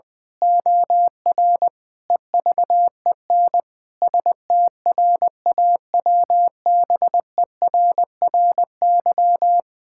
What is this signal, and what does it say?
Morse code, heard as a keyed tone, about 20 wpm, 700 Hz. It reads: OR EVEN STRAWBERRY
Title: Pangram Challenge v2 - 20wpm